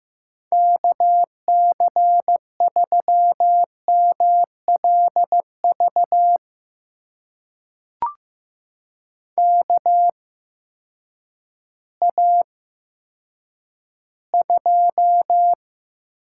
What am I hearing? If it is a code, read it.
Morse code, heard as a keyed tone, about 15 wpm, spaced as for 9 wpm, 700 Hz. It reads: KC3MLV KA2
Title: USA Call Signs - 6 Characters - 8x Character Spacing - 15wpm